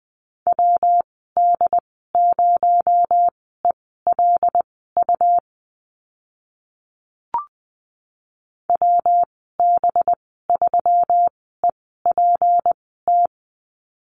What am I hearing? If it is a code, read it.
Morse code, heard as a keyed tone, about 20 wpm, spaced as for 14 wpm, 700 Hz. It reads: WD0ELU WB3EPT
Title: 219 Lesson 35g - Number 0 - Call Signs 20wpm